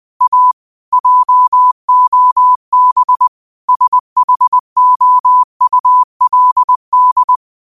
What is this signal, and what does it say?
Morse code, heard as a keyed tone, about 20 wpm, 1000 Hz. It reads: A JOB SHOULD
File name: qotdfiles / 20wpm